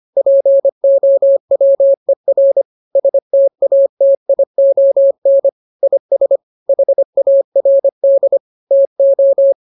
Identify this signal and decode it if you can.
Morse code, heard as a keyed tone, about 25 wpm, 550 Hz. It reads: POWER STATION IS HARD TO